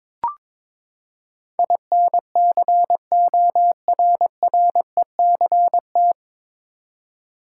Morse code, heard as a keyed tone, about 22 wpm, 700 Hz. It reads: INCORRECT
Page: Morse Code Ninja practice 22wpm